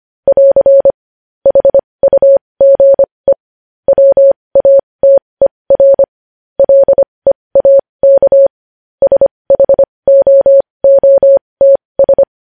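Morse code, heard as a keyed tone, about 25 wpm, spaced as for 20 wpm, 550 Hz. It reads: <AR> HUGE WATER LEAK SHOOTS
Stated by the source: Morse code news headlines at 20 wpm.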